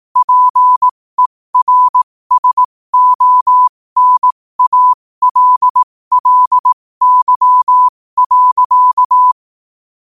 Morse code, heard as a keyed tone, about 18 wpm, spaced as for 15 wpm, 1000 Hz. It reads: PERSONALLY.
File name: qotdfiles / 15wpm